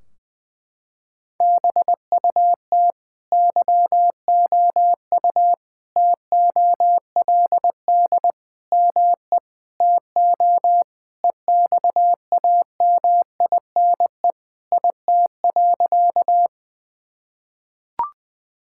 Morse code, heard as a keyed tone, about 20 wpm, 700 Hz. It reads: BUT YOU TOLD ME TO EXAMINE IT.
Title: Sentences from Top 1600 Words 20wpm